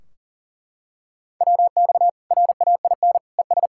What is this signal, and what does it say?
Morse code, heard as a keyed tone, about 40 wpm, 700 Hz. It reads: WX RAIN ES
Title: Practice QSO - Exchange 2 - Info - 40wpm